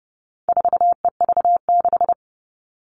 Morse code, heard as a keyed tone, about 30 wpm, 700 Hz. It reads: 4EV6
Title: All Cut Numbers - 4-Digit 30wpm